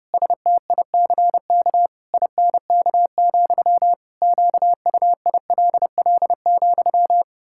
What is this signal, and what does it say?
Morse code, heard as a keyed tone, about 30 wpm, 700 Hz. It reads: STICK INK, QUILL,
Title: Pangram Challenge v2 - 30wpm